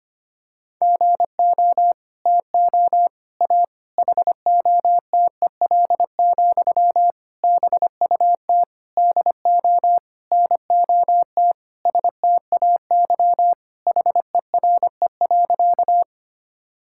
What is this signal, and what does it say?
Morse code, heard as a keyed tone, about 25 wpm, 700 Hz. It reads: GO TO A HOTEL, BUT DO NOT STAY HERE.
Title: Sentences from Top 1600 Words 25wpm